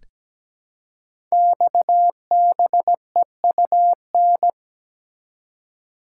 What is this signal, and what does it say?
Morse code, heard as a keyed tone, about 17 wpm, 700 Hz. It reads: XBEUN
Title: Code Groups - 5 Letters - 17wpm